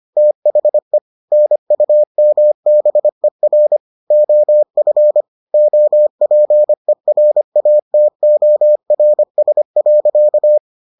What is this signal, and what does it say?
Morse code, heard as a keyed tone, about 25 wpm, 600 Hz. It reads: THE NUMBER OF OPERATORS.